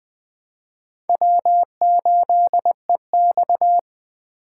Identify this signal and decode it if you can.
Morse code, heard as a keyed tone, about 20 wpm, 700 Hz. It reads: W8EX
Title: USA Call Signs - 4 Characters - 3x Character Spacing - 20wpm